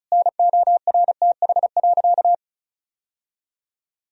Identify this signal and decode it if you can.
Morse code, heard as a keyed tone, about 36 wpm, 700 Hz. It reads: NORTH.
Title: Sentences from Top 1000 Words - Review Part 1 - 35wpm